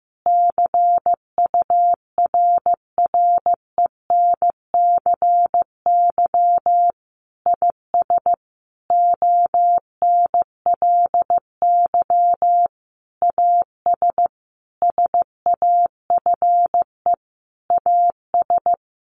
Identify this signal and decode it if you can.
Morse code, heard as a keyed tone, about 15 wpm, 700 Hz. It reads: CURRENCY IS ONLY AS SAFE AS
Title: Sentences from Top 1500 Words 15wpm